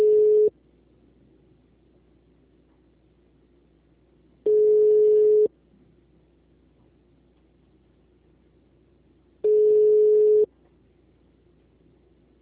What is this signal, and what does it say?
Ringback tone (Europe, 425 Hz).